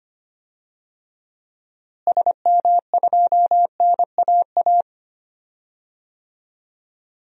Morse code, heard as a keyed tone, about 25 wpm, 700 Hz. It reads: SM2NAA
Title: DXCC-100 Call Signs - 6 Characters - Repeated 3x - 25wpm